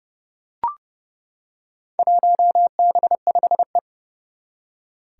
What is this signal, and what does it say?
Morse code, heard as a keyed tone, about 30 wpm, 700 Hz. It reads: 1B5E
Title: Four Challenging Letter-Number Combinations 30wpm